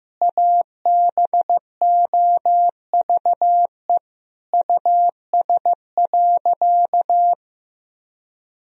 Morse code, heard as a keyed tone, about 15 wpm, 700 Hz. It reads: ABOVE US.
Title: Sentences from Top 1000 Words - Review Part 2 - 15wpm